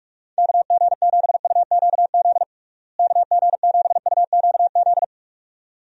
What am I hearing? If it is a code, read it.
Morse code, heard as a keyed tone, about 47 wpm, 700 Hz. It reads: KG7UQZ KG7UQZ
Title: USA Call Signs - 6 Characters - Repeated 2x - 45wpm